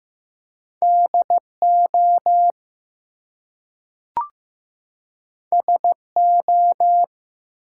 Morse code, heard as a keyed tone, about 15 wpm, 700 Hz. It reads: DO SO
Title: Two Letter Words - 15wpm